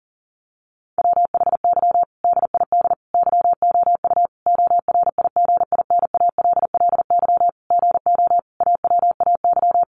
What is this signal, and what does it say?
Morse code, heard as a keyed tone, about 40 wpm, 700 Hz. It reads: WHY DID YOU ORIGINALLY GO AWAY
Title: Sentences from Top 1500 Words 40wpm